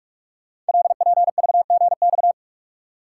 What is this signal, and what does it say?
Morse code, heard as a keyed tone, about 44 wpm, 700 Hz. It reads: LPVGX
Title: Code Groups - 5 Letters - 45wpm